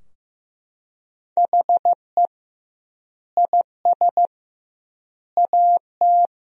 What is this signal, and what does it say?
Morse code, heard as a keyed tone, about 15 wpm, 700 Hz. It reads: HE IS AT
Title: 2x Word-Spacing - Sentences from Top 200 Words - 15wpm